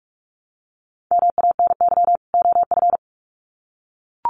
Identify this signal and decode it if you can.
Morse code, heard as a keyed tone, about 44 wpm, 700 Hz. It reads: MANY OF
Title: Sets of 2 Words 45wpm